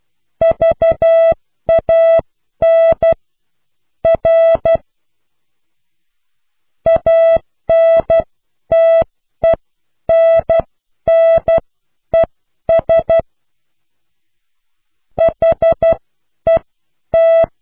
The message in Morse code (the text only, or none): VAN R ANTENNES HET